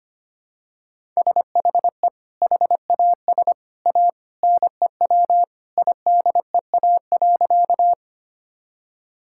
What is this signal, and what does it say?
Morse code, heard as a keyed tone, about 25 wpm, 700 Hz. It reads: SHE HAS A NEW IDEA.
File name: Sentences with 2-Word Phrases - 25wpm